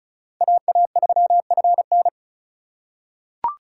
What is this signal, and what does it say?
Morse code, heard as a keyed tone, about 36 wpm, 700 Hz. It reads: AA3FN
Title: USA Call Signs - 5 Characters - 4x Character Spacing - 35wpm